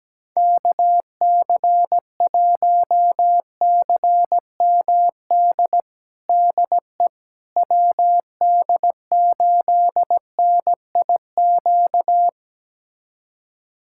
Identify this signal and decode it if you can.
Morse code, heard as a keyed tone, about 17 wpm, 700 Hz. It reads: KC1CMD DE WD8NIQ
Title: USA Callsign DE Callsign 17wpm